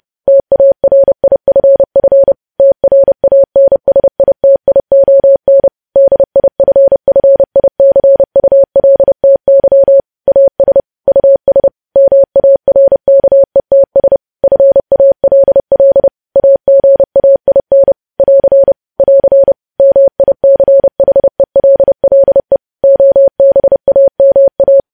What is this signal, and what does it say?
Morse code, heard as a keyed tone, about 30 wpm, 550 Hz. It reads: TARIFF TRANSITION DIFFICULTY AS US MARKETS FALL AGAIN <AR> <AR> MICHELLE OBAMA
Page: news headlines in morse code at 30 wpm